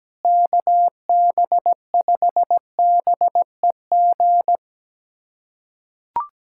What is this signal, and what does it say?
Morse code, heard as a keyed tone, about 17 wpm, 700 Hz. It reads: KB5BEG
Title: USA Callsign DE Callsign - 6 Characters - 17wpm